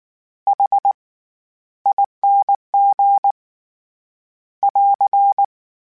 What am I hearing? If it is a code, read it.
Morse code, heard as a keyed tone, about 19 wpm, spaced as for 13 wpm, 800 Hz. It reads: HING <AR>
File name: Special INGs / Special INGs-mp3s / SlowToFaster ing @18